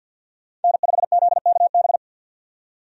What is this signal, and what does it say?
Morse code, heard as a keyed tone, about 49 wpm, 700 Hz. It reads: N5ZKB
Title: USA Call Signs - 5 Characters - 50wpm